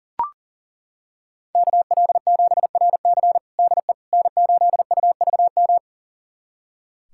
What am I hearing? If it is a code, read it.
Morse code, heard as a keyed tone, about 40 wpm, 700 Hz. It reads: KL7RC DE N8UVM
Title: USA Callsign DE Callsign - 5 Characters - 40wpm